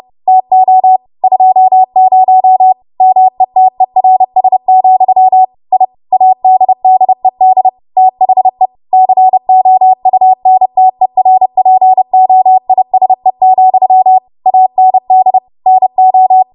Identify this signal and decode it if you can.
Morse code, heard as a keyed tone, about 30 wpm, 750 Hz. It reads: TO 20 METERS, I ADDED THE COUNTERPOISE, AND NO
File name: Morse / 30WPM